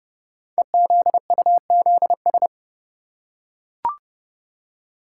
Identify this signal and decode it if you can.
Morse code, heard as a keyed tone, about 28 wpm, 700 Hz. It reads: EZUZS
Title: Code Groups - 5 Letters - 30wpm